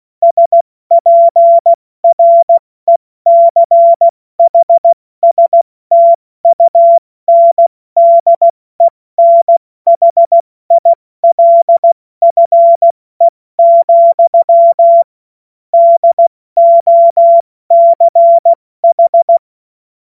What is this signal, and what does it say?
Morse code, heard as a keyed tone, about 16 wpm, spaced as for 14 wpm, 680 Hz. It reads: SPRECHSTUNDENHILFE, DOCH
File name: Morsekurs / Witze / Witz 12 März 2024 70 F80